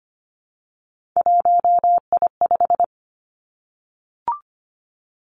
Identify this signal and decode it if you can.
Morse code, heard as a keyed tone, about 25 wpm, 700 Hz. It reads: 1I5
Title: Three Challenging Letter-Number Combinations 25wpm